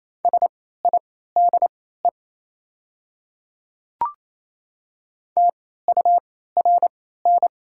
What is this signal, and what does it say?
Morse code, heard as a keyed tone, about 28 wpm, spaced as for 16 wpm, 700 Hz. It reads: SIDE TURN
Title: 3x Character Spacing - Top 200 Words - 28wpm